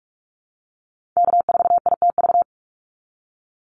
Morse code, heard as a keyed tone, about 44 wpm, 700 Hz. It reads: K4ITV
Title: Call Signs Repeated 3x - 45wpm